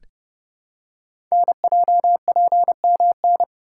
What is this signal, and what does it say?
Morse code, heard as a keyed tone, about 30 wpm, 700 Hz. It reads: NJPMN
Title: Code Groups - 5 Letters - 30wpm